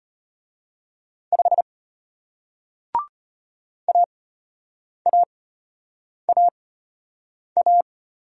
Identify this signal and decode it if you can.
Morse code, heard as a keyed tone, about 33 wpm, spaced as for 8 wpm, 700 Hz. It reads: 5 AAAA